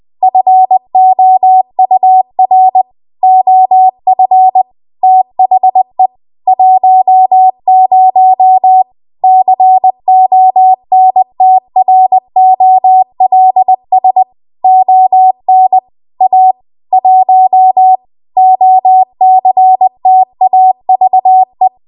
Morse code, heard as a keyed tone, about 20 wpm, 750 Hz. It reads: FOUR OF THE 10 CONTROLS ON A 1 OCTAVE